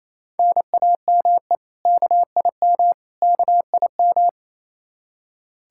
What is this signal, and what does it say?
Morse code, heard as a keyed tone, about 27 wpm, 700 Hz. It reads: NAME KIM KIM
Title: K1USN SST QSO Practice 28wpm